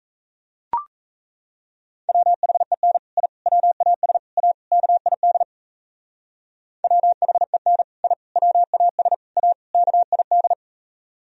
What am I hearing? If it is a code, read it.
Morse code, heard as a keyed tone, about 36 wpm, 700 Hz. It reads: WHEN I WAS A KID WHEN I WAS A KID